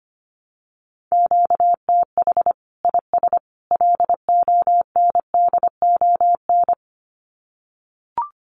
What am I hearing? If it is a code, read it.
Morse code, heard as a keyed tone, about 25 wpm, 700 Hz. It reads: QTH IS LONDON
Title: Common QSO Phrases 25wpm